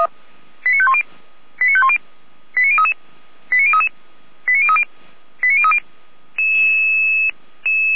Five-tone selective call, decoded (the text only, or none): 89510 89510 89033 89033 89030 89030